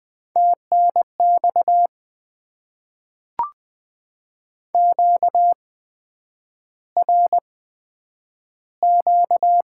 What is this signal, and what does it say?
Morse code, heard as a keyed tone, about 20 wpm, spaced as for 9 wpm, 700 Hz. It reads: TNX QRQ
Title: Top 90 QSO Elements - 8x Character Spacing - 20wpm